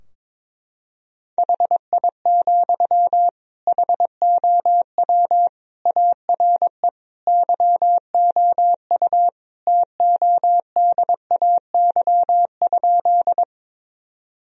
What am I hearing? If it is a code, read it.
Morse code, heard as a keyed tone, about 22 wpm, 700 Hz. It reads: HI, HOW ARE YOU TODAY?
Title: Sentences from Top 1000 Words - Review Part 3 - 22wpm